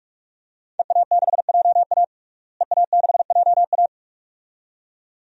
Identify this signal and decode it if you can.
Morse code, heard as a keyed tone, about 48 wpm, 700 Hz. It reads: EA6JA EA6JA
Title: DXCC-100 Call Signs - 5 Characters - Repeated 2x - 45wpm